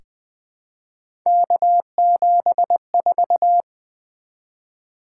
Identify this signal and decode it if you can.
Morse code, heard as a keyed tone, about 20 wpm, 700 Hz. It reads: K74
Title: Three Letter-Number Combinations - 20wpm